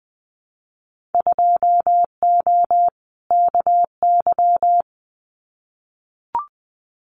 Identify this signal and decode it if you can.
Morse code, heard as a keyed tone, about 20 wpm, 700 Hz. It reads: 2O KY
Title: Winter Field Day Exchanges 20wpm